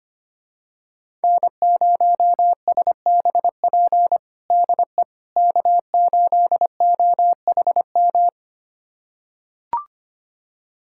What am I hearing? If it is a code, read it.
Morse code, heard as a keyed tone, about 25 wpm, 700 Hz. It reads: N0SBP DE K8OHM
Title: USA Callsign DE Callsign - 5 Characters - 25wpm